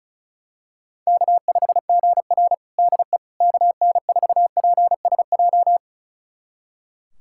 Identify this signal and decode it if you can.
Morse code, heard as a keyed tone, about 34 wpm, 700 Hz. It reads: K5GR DE KN4PSJ
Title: USA Callsign DE Callsign 35wpm